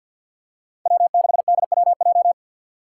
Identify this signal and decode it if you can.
Morse code, heard as a keyed tone, about 50 wpm, 700 Hz. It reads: W6DWJ
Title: Five Challenging Letter-Number Combinations 50wpm